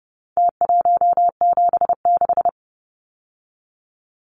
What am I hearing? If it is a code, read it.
Morse code, heard as a keyed tone, about 30 wpm, 700 Hz. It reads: T176
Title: Numbers - Contest Style - 4-Digit 30wpm